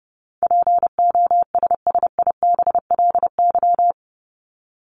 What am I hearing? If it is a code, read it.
Morse code, heard as a keyed tone, about 30 wpm, 700 Hz. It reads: POSSIBLY
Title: LY Suffix - Top 100 Words with Base Words 30wpm